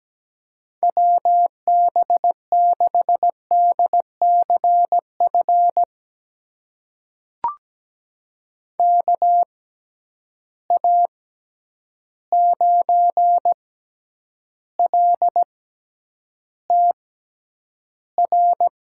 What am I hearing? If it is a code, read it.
Morse code, heard as a keyed tone, about 17 wpm, spaced as for 9 wpm, 700 Hz. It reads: WB6DCF KA9LTR
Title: USA Call Signs - 6 Characters - 6x Character Spacing - 17wpm